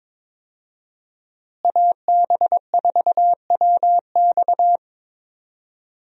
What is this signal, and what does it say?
Morse code, heard as a keyed tone, about 22 wpm, 700 Hz. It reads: AB4WX
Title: USA Call Signs - 5 Characters - Repeated 3x - 22wpm